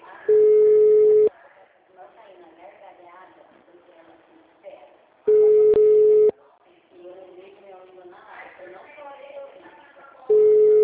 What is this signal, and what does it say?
Ringback tone (Europe, 425 Hz), with speech.